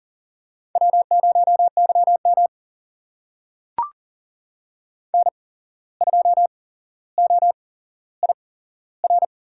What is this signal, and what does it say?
Morse code, heard as a keyed tone, about 38 wpm, spaced as for 16 wpm, 700 Hz. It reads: W0YM N2OIR
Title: USA Call Signs - 8x Character Spacing - 40wpm